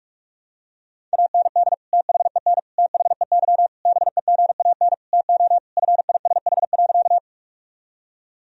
Morse code, heard as a keyed tone, about 46 wpm, 700 Hz. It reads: AND THEN THEY BEGAN TO FISH.
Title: Sentences from Top 1000 Words - Review Part 1 - 45wpm